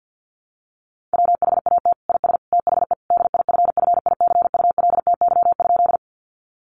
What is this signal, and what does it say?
Morse code, heard as a keyed tone, about 50 wpm, 700 Hz. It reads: WHAT IS THE DIFFICULTY?
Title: Sentences from Top 1500 Words 50wpm